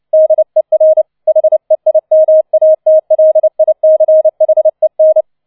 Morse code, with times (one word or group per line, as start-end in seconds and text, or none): 0.1-1.0 DER
1.3-5.2 HEIMATLICHEN